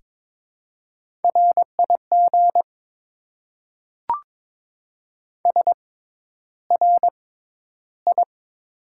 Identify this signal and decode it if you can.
Morse code, heard as a keyed tone, about 22 wpm, spaced as for 11 wpm, 700 Hz. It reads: RIG SRI
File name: Top 90 QSO Elements - 6x Character Spacing - 22wpm